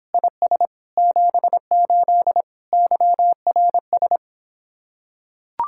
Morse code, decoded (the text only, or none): IS 78 YRS